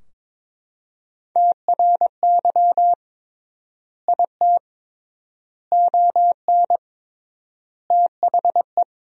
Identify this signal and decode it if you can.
Morse code, heard as a keyed tone, about 22 wpm, 700 Hz. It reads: TRY IT ON THE
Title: 3x Word-Spacing - Sentences from Top 400 Words - 22wpm